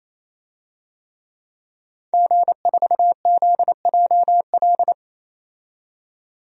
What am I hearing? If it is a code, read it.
Morse code, heard as a keyed tone, about 28 wpm, 700 Hz. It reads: G4ZJL
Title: DXCC-100 Call Signs - 5 Characters - 28wpm